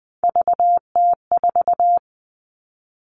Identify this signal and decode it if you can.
Morse code, heard as a keyed tone, about 20 wpm, 700 Hz. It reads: VT4
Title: All Cut Numbers - 3-Digit 20wpm